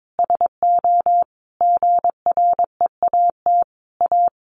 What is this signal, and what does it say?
Morse code, heard as a keyed tone, about 22 wpm, 700 Hz. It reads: SO GREAT A
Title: Sentences from Top 1200 Words 22wpm